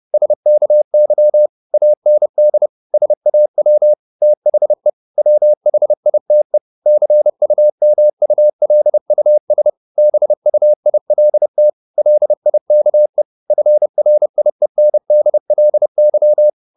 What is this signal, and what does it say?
Morse code, heard as a keyed tone, about 30 wpm, 600 Hz. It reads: SKY AND SAW THE WHITE CUMULUS BUILT LIKE FRIENDLY
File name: the old man and the sea / the old man and the sea 1 30wpm